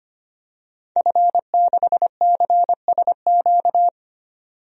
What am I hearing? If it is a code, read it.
Morse code, heard as a keyed tone, about 25 wpm, 700 Hz. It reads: F6CSQ
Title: DXCC-100 Call Signs - 5 Characters - 25wpm